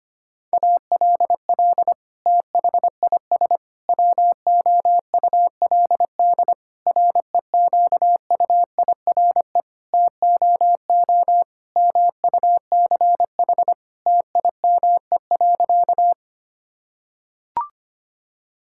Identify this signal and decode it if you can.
Morse code, heard as a keyed tone, about 25 wpm, 700 Hz. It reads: ALL THIS WOULD REQUIRE TOO MUCH TIME.
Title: Sentences from Top 1000 Words - Review Part 6 - 25wpm